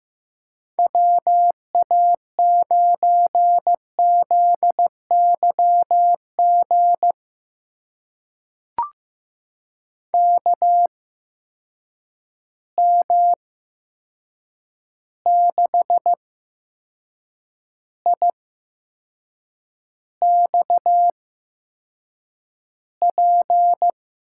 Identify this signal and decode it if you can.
Morse code, heard as a keyed tone, about 15 wpm, spaced as for 6 wpm, 700 Hz. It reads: WA9ZYG KM6IXP
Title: USA Call Signs - 6 Characters - 8x Character Spacing - 15wpm